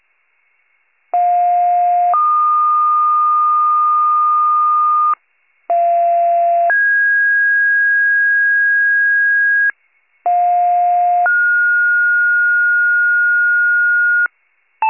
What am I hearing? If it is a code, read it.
Two-tone page, heard as a one-second tone, then a three-second tone, about 710 then 1220 Hz, about 690 then 1650 Hz, about 710 then 1430 Hz, with noise. It is heard from 1.1 to 14.3 s.